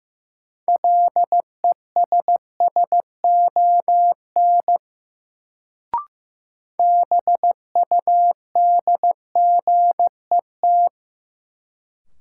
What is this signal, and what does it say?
Morse code, heard as a keyed tone, about 15 wpm, 700 Hz. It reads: LESSON BUDGET
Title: Sentences from Top 1200 Words - New Word Drill Alt - 15wpm